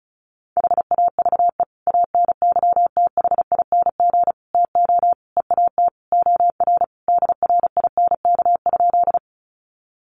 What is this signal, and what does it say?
Morse code, heard as a keyed tone, about 34 wpm, 700 Hz. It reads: HAVE ANYTHING TO EAT OR DRINK?
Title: Sentences from Top 1500 Words 35wpm